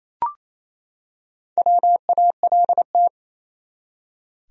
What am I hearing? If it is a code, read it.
Morse code, heard as a keyed tone, about 28 wpm, 700 Hz. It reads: WALT
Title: USA Common Names - Max 5 Letters - 28wpm